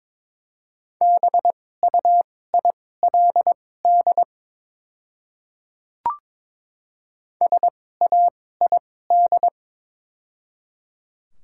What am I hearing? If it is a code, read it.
Morse code, heard as a keyed tone, about 22 wpm, spaced as for 16 wpm, 700 Hz. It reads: BUILD SAID
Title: 2x Character Spacing - Top 200 Words - 22wpm